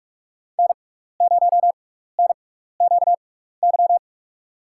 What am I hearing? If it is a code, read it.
Morse code, heard as a keyed tone, about 45 wpm, spaced as for 15 wpm, 700 Hz. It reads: N0NQY
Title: USA Call Signs - 5 Characters - 6x Character Spacing - 45wpm